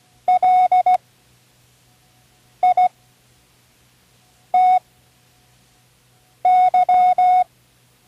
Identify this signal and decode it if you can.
Morse code, heard as a keyed tone, about 16 wpm, spaced as for 5 wpm, 730 Hz. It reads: LITY
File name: Learning Morse Code (from Tune in the World)